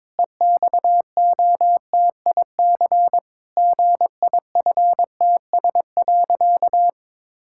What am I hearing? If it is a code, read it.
Morse code, heard as a keyed tone, about 22 wpm, 700 Hz. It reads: EXOTIC GIFTS.
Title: Pangram Challenge v2 - 22wpm